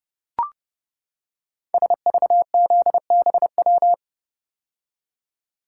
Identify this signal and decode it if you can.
Morse code, heard as a keyed tone, about 29 wpm, 690 Hz. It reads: SVZBW
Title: Code Groups - 5 Letters - 30wpm